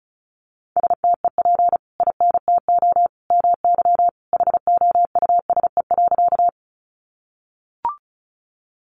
Morse code, heard as a keyed tone, about 34 wpm, 700 Hz. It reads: STEP INTO MY HOUSE.
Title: Sentences from Top 600 Words 35wpm